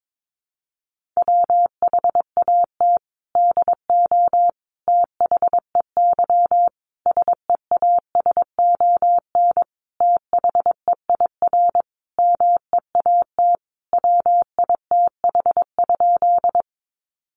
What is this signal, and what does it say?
Morse code, heard as a keyed tone, about 22 wpm, 700 Hz. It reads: WHAT DO THEY SEASON THEIR MEAT WITH?
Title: Sentences from Top 1400 Words 22wpm